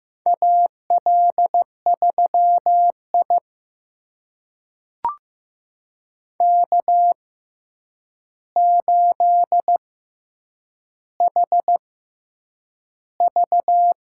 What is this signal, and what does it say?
Morse code, heard as a keyed tone, about 15 wpm, spaced as for 8 wpm, 700 Hz. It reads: AL3I K8HV
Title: USA Call Signs - 4 Characters - 6x Character Spacing - 15wpm